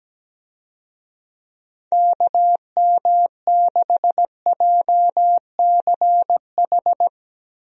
Morse code, heard as a keyed tone, about 17 wpm, 700 Hz. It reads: KM6JCH
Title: USA Call Signs - 6 Characters - 17wpm